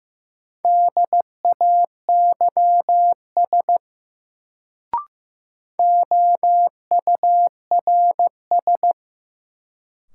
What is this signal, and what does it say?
Morse code, heard as a keyed tone, about 15 wpm, 700 Hz. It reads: DAYS OURS
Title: Sentences from Top 1000 Words - Review Part 5 - New Word Drill - 15wpm